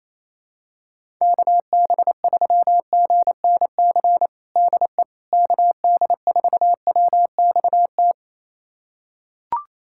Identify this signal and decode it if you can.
Morse code, heard as a keyed tone, about 28 wpm, 700 Hz. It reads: KB3GNC DE KD4WXT
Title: USA Callsign DE Callsign - 6 Characters - 28wpm